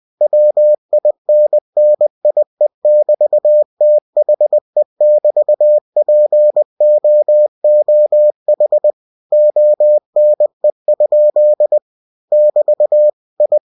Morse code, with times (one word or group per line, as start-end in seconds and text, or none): 0.2-8.9 WINNIE<BT>THE<BT>POOH
9.3-11.8 ONE?
12.3-13.1 <BT>
13.4-13.6 I